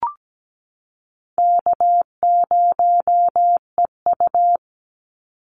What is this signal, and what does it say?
Morse code, heard as a keyed tone, about 17 wpm, 700 Hz. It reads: K0EU
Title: Easy Call Signs Repeated 2x - 17wpm